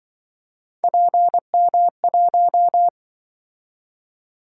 Morse code, heard as a keyed tone, about 25 wpm, 700 Hz. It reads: PM1